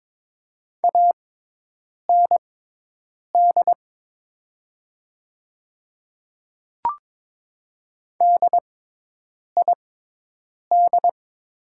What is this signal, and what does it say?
Morse code, heard as a keyed tone, about 22 wpm, spaced as for 8 wpm, 700 Hz. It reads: AND DID
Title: 6x Character Spacing - Top 100 Words - 22wpm